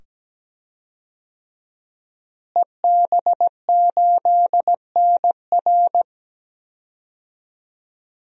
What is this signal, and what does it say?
Morse code, heard as a keyed tone, about 17 wpm, 700 Hz. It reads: EB8NR
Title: DXCC-100 Call Signs - 5 Characters - Repeated 2x - 17wpm